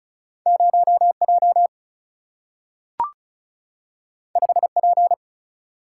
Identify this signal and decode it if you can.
Morse code, heard as a keyed tone, about 36 wpm, 700 Hz. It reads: 0J 5P
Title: Single-Letter Single-Number 35wpm